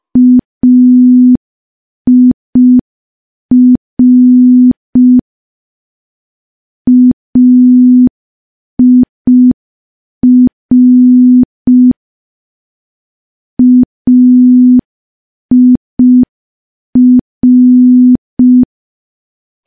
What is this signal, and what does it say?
Morse code, heard as a keyed tone, about 5 wpm, 250 Hz. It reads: AIR AIR AIR
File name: morse / morse-training-05WPM